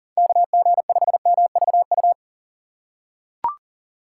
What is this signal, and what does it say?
Morse code, heard as a keyed tone, about 40 wpm, 700 Hz. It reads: KG5MVU
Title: USA Call Signs - 6 Characters - 3x Character Spacing - 40wpm